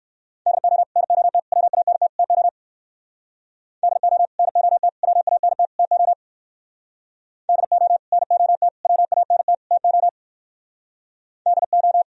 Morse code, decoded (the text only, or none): DO NOT WANT TO DO NOT WANT TO DO NOT WANT TO DO